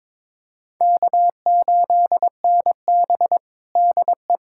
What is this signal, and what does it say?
Morse code, heard as a keyed tone, about 22 wpm, 700 Hz. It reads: K8NB DE
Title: USA Callsign DE Callsign - 4 Characters - 22wpm